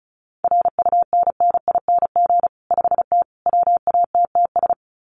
Morse code, heard as a keyed tone, about 34 wpm, 700 Hz. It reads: RUNNING 5T WATTS
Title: Common QSO Phrases 35wpm